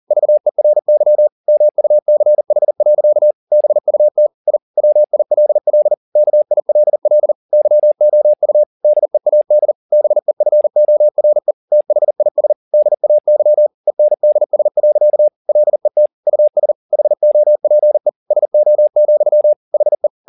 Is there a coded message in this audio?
Yes: Morse code from 0.1 to 20.1 s, heard as a keyed tone, about 40 wpm, 600 Hz. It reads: VERY MUCH. BUT I WILL KILL YOU DEAD BEFORE THIS DAY ENDS. LET US HOPE SO, HE